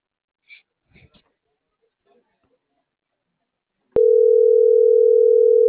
Ringback tone (North America), 440 + 480 Hz.
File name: CENTRAL MAERSK / out polanco.msk - .46487